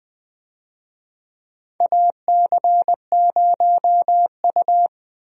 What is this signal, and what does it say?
Morse code, heard as a keyed tone, about 20 wpm, 700 Hz. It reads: AC0U
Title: USA Call Signs - 4 Characters - 20wpm